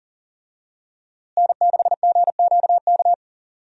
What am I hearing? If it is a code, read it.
Morse code, heard as a keyed tone, about 40 wpm, 700 Hz. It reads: N6GQK
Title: USA Call Signs - 5 Characters - Repeated 2x - 40wpm